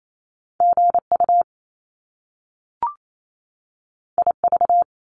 Morse code, heard as a keyed tone, about 28 wpm, 700 Hz. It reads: GU IV
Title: Top Two Letter Combinations in Words 28wpm